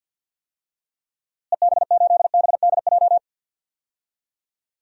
Morse code, heard as a keyed tone, about 51 wpm, 700 Hz. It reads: EB8BDJ
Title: DXCC-100 Call Signs - 6 Characters - Repeated 3x - 50wpm